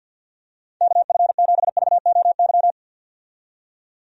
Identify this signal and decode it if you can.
Morse code, heard as a keyed tone, about 46 wpm, 700 Hz. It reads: KF7VOY
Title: USA Call Signs - 6 Characters - 50wpm